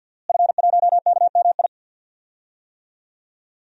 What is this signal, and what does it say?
Morse code, heard as a keyed tone, about 46 wpm, 700 Hz. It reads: F1KMI
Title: DXCC-100 Call Signs - 5 Characters - Repeated 2x - 50wpm